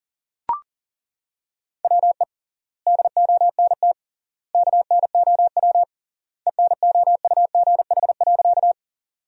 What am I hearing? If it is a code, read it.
Morse code, heard as a keyed tone, about 37 wpm, 700 Hz. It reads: WE DONT KNOW ENOUGH.
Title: 3x Word-Spacing - Sentences from Top 400 Words - 40wpm